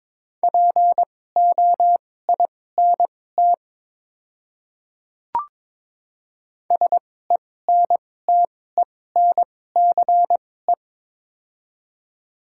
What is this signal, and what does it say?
Morse code, heard as a keyed tone, about 22 wpm, spaced as for 16 wpm, 700 Hz. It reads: POINT SENTENCE
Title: 2x Character Spacing - Top 200 Words - 22wpm